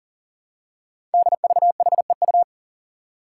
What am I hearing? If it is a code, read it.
Morse code, heard as a keyed tone, about 39 wpm, 700 Hz. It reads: DVHEU
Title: Five Challenging Letter-Number Combinations 40wpm